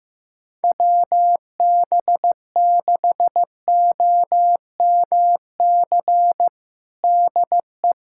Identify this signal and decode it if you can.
Morse code, heard as a keyed tone, about 15 wpm, 700 Hz. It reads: WB6OMC DE
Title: USA Callsign DE Callsign 15wpm